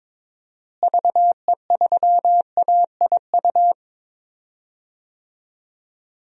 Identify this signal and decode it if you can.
Morse code, heard as a keyed tone, about 22 wpm, 700 Hz. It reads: VE3AIU
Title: DXCC-100 Call Signs - 6 Characters - Repeated 2x - 22wpm